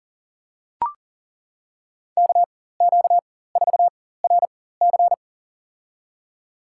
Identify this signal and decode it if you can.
Morse code, heard as a keyed tone, about 40 wpm, spaced as for 19 wpm, 700 Hz. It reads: KQ4RC
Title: USA Call Signs - 5 Characters - 4x Character Spacing - 40wpm